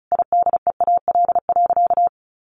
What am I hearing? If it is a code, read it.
Morse code, heard as a keyed tone, about 34 wpm, 700 Hz. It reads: IDEAL.
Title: Sentences from Top 1400 Words 35wpm